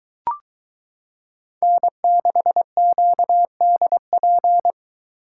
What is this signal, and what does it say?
Morse code, heard as a keyed tone, about 22 wpm, 700 Hz. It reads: N6QDP